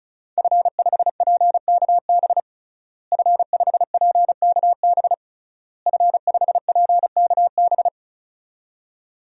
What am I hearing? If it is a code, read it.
Morse code, heard as a keyed tone, about 36 wpm, 700 Hz. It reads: F5PKB F5PKB F5PKB
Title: DXCC-100 Call Signs - 5 Characters - Repeated 3x - 35wpm